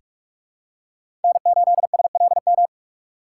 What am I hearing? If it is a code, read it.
Morse code, heard as a keyed tone, about 46 wpm, 700 Hz. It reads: N8SLM
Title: USA Call Signs - 5 Characters - 3x Character Spacing - 45wpm